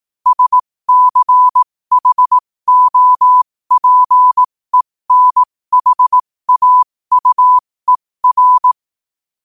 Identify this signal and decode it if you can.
Morse code, heard as a keyed tone, about 18 wpm, spaced as for 15 wpm, 1000 Hz. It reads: SCHOPENHAUER
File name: qotdfiles / 15wpm